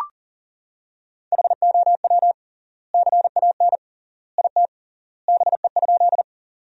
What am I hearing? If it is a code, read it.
Morse code, heard as a keyed tone, about 40 wpm, 700 Hz. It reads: HOW CAN IT BE?
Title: 3x Word-Spacing - Sentences from Top 100 Words - 40wpm